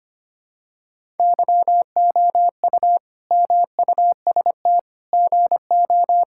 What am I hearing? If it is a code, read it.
Morse code, heard as a keyed tone, about 25 wpm, 700 Hz. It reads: YOU MUST GO
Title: Sentences from Top 1000 Words - Review Part 5 - 25wpm